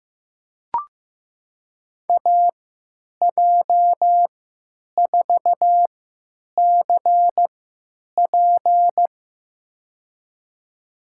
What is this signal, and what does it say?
Morse code, heard as a keyed tone, about 15 wpm, spaced as for 9 wpm, 700 Hz. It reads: AJ4CP
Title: USA Call Signs - 5 Characters - 3x Character Spacing - 15wpm